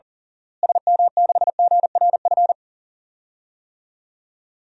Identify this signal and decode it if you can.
Morse code, heard as a keyed tone, about 40 wpm, 700 Hz. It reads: SM6GRF
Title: DXCC-100 Call Signs - 6 Characters - Repeated 2x - 40wpm